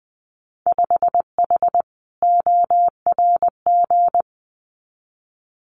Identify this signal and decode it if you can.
Morse code, heard as a keyed tone, about 20 wpm, 700 Hz. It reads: 5H ORG
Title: Winter Field Day Exchanges Encore 20wpm